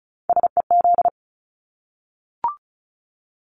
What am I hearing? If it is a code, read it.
Morse code, heard as a keyed tone, about 34 wpm, 700 Hz. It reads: SEZ
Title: Abbreviations Repeated 3x - 35wpm